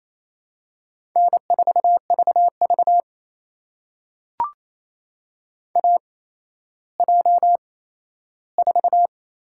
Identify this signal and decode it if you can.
Morse code, heard as a keyed tone, about 28 wpm, spaced as for 14 wpm, 700 Hz. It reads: N4VV AJ4
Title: USA Call Signs - 8x Character Spacing - 28wpm